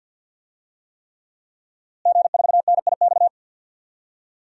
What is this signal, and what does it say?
Morse code, heard as a keyed tone, about 51 wpm, 700 Hz. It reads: G4NIX
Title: DXCC-100 Call Signs - 5 Characters - Repeated 3x - 50wpm